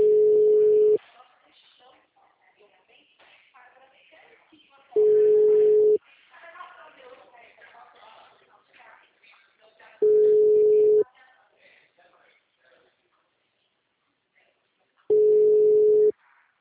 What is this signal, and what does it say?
Ringback tone (Europe, 425 Hz), with speech.